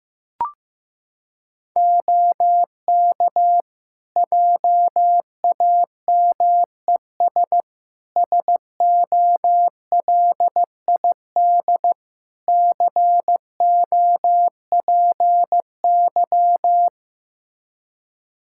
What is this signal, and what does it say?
Morse code, heard as a keyed tone, about 15 wpm, 700 Hz. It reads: OK JAMES SOLID COPY
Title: Practice QSO - Exchange 3 - More Info - 15wpm